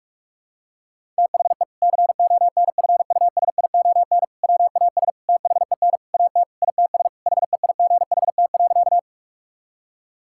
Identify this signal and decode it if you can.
Morse code, heard as a keyed tone, about 46 wpm, 700 Hz. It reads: THE CONFUSION WAS THEN AT ITS HEIGHT.
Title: Sentences from Top 1600 Words 45wpm